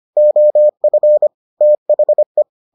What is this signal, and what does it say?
Morse code, heard as a keyed tone, about 25 wpm, 600 Hz. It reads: OF THE